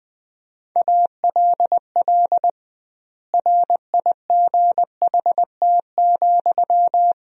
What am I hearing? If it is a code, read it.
Morse code, heard as a keyed tone, about 21 wpm, 700 Hz. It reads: ALL RIGHT,
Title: 2x Word-Spacing - Sentences from Top 200 Words - 20wpm